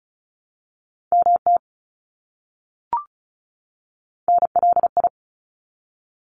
Morse code, heard as a keyed tone, about 34 wpm, 700 Hz. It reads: MT NLI
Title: ARRL Sections 35wpm